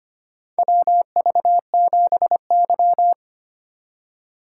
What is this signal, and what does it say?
Morse code, heard as a keyed tone, about 25 wpm, 700 Hz. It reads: WV7Y
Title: USA Call Signs - 4 Characters - 2x Character Spacing - 25wpm